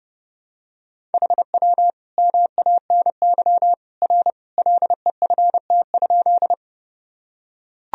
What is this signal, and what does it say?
Morse code, heard as a keyed tone, about 32 wpm, 700 Hz. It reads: HW MANY R LEFT?
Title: Sentences from Top 300 Words and Abbreviations 30wpm